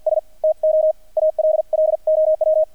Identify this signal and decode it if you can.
Morse code, heard as a keyed tone, about 50 wpm, spaced as for 43 wpm, 630 Hz, with radio noise. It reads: S TO ALLOW
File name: archived / CRSnet